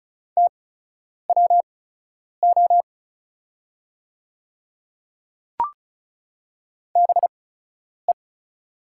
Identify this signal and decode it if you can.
Morse code, heard as a keyed tone, about 36 wpm, spaced as for 10 wpm, 700 Hz. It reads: TWO BE
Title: 8x Character Spacing - Top 100 Words - 35wpm